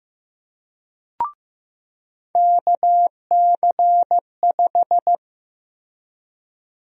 Morse code, heard as a keyed tone, about 15 wpm, 700 Hz. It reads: KC5
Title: USA Call Sign Prefixes - Repeated 2x - 15wpm